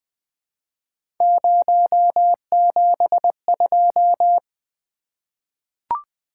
Morse code, heard as a keyed tone, about 20 wpm, 700 Hz. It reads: 072
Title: Three Letter-Number Combinations - 20wpm